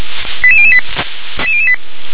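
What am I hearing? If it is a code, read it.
Five-tone selective call (ZVEI), heard as quick runs of five tones, tones of about 70 ms. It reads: 88008
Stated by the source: Five-tone ZVEI selective call 88083, received on SDR.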